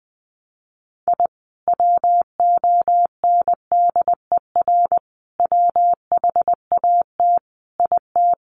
Morse code, heard as a keyed tone, about 20 wpm, 700 Hz. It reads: I WONDER WHAT IT
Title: Sentences from Top 600 Words 20wpm